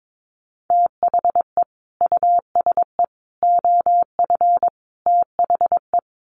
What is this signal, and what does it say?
Morse code, heard as a keyed tone, about 22 wpm, 700 Hz. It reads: THE USE OF THE
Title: Sets of 4 Words 22wpm